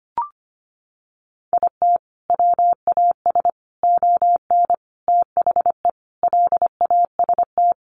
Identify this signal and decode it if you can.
Morse code, heard as a keyed tone, about 25 wpm, 700 Hz. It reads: IT WAS ON THE LAST
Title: Sentences from Top 1200 Words 25wpm